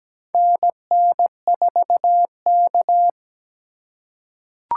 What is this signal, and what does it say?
Morse code, heard as a keyed tone, about 17 wpm, 700 Hz. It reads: NN4K
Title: USA Call Signs - 4 Characters - 2x Character Spacing - 17wpm